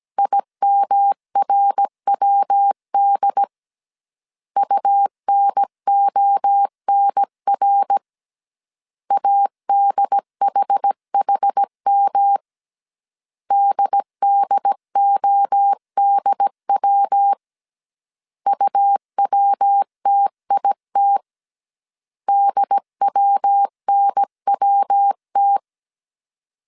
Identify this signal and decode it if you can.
Morse code, heard as a keyed tone, about 17 wpm, 800 Hz. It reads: IMRWD UNONR ADHHM DDODW UWTIT DWNWT